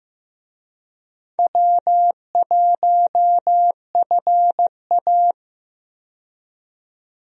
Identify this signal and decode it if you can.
Morse code, heard as a keyed tone, about 15 wpm, 700 Hz. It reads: W1FA
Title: USA Call Signs - 4 Characters - 15wpm